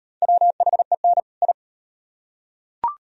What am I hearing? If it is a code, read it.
Morse code, heard as a keyed tone, about 33 wpm, 700 Hz. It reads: WHEN I